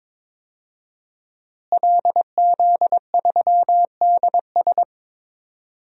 Morse code, heard as a keyed tone, about 22 wpm, 700 Hz. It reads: LZ3DS
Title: DXCC-100 Call Signs - 5 Characters - Repeated 2x - 22wpm